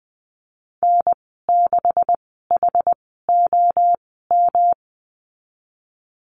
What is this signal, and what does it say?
Morse code, heard as a keyed tone, about 20 wpm, spaced as for 15 wpm, 700 Hz. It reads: N6HOM
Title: 140 Lesson 23g - Letter f - Call Signs 20wpm